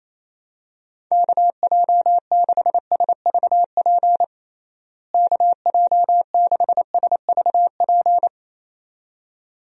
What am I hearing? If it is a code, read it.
Morse code, heard as a keyed tone, about 27 wpm, 700 Hz. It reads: KJ6SVP KJ6SVP
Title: USA Call Signs - 6 Characters - Repeated 2x - 28wpm